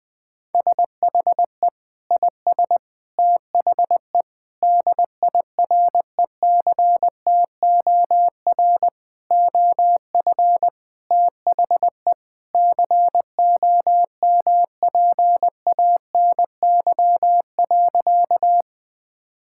Morse code, heard as a keyed tone, about 21 wpm, 700 Hz. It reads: SHE IS THE DIRECTOR OF THE COMPANY.
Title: Sentences with 3-Word Phrases - 20wpm